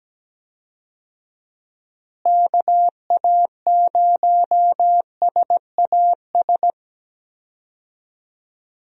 Morse code, heard as a keyed tone, about 17 wpm, 700 Hz. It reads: KA0SAS
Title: USA Call Signs - 6 Characters - Repeated 3x - 17wpm